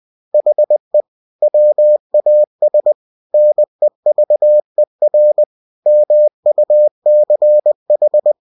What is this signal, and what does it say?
Morse code, heard as a keyed tone, about 20 wpm, 600 Hz. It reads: HE WAS NEVER MUCH
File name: cw-books / winnie-the-pooh / winnie-20wpm-page-0045